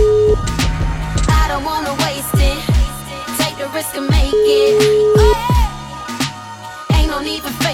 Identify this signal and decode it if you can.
Ringback tone (Europe, 425 Hz), with noise.